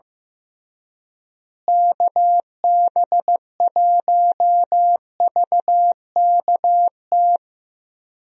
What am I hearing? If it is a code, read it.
Morse code, heard as a keyed tone, about 15 wpm, 700 Hz. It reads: KB1VKT
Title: USA Call Signs - 6 Characters - Repeated 3x - 15wpm